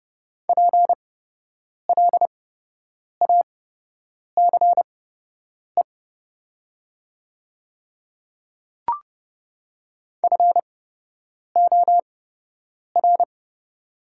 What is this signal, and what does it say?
Morse code, heard as a keyed tone, about 30 wpm, spaced as for 8 wpm, 700 Hz. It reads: PLACE FOR
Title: 8x Character Spacing - Top 200 Words - 30wpm